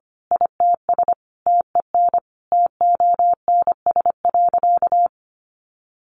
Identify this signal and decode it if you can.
Morse code, heard as a keyed tone, about 25 wpm, 700 Hz. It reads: ITS TEN TONS.
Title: A Sound Of Thunder 25wpm